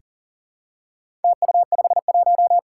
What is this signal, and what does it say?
Morse code, heard as a keyed tone, about 39 wpm, 700 Hz. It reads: TU51
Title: Four Challenging Letter-Number Combinations 40wpm